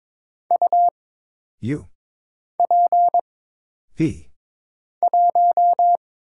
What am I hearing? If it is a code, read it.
Morse code, heard as a keyed tone, about 22 wpm, 700 Hz. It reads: U P 1
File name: Single Character (Letter:Number:Punctuation:Prosign) - Mind-Melt - 22wpm